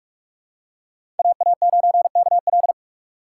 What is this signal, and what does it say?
Morse code, heard as a keyed tone, about 47 wpm, 700 Hz. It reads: AA9KL
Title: USA Call Signs - 5 Characters - 3x Character Spacing - 45wpm